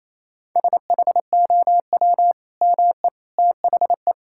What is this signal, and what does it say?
Morse code, heard as a keyed tone, about 28 wpm, 700 Hz. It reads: SHOW ME THE
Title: Sentences from Top 1600 Words 28wpm